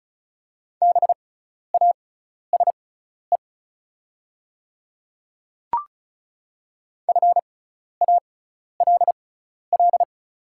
Morse code, heard as a keyed tone, about 34 wpm, spaced as for 12 wpm, 700 Hz. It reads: BASE FALL
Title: 6x Character Spacing - Top 500 Words - 35wpm